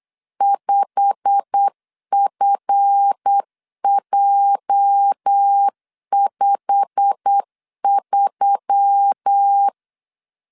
Morse code, heard as a keyed tone, about 8 wpm, 800 Hz. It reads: 5FJ53
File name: LEZIONI / 8 YJ45 / 8 YJ45 8wpm 3214